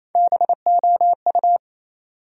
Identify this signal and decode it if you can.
Morse code, heard as a keyed tone, about 28 wpm, 700 Hz. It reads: BOU
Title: USA Call Sign Postfixes - 3 Letters 28wpm